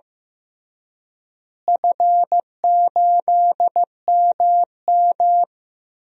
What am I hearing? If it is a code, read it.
Morse code, heard as a keyed tone, about 15 wpm, 700 Hz. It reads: F8MM
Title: DXCC-100 Call Signs - 4 Characters - Repeated 3x - 15wpm